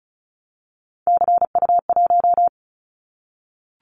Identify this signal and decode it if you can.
Morse code, heard as a keyed tone, about 36 wpm, 700 Hz. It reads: CU1
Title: Callsign Prefixes for Top 100 DXCC - 35wpm